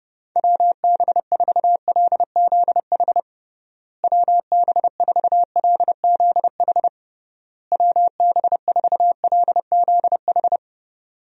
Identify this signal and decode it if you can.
Morse code, heard as a keyed tone, about 30 wpm, 700 Hz. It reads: WB4LZH WB4LZH WB4LZH
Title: USA Call Signs - 6 Characters - Repeated 3x - 30wpm